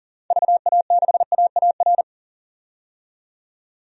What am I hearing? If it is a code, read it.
Morse code, heard as a keyed tone, about 37 wpm, 700 Hz. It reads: VA6AAR
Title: DXCC-100 Call Signs - 6 Characters - 40wpm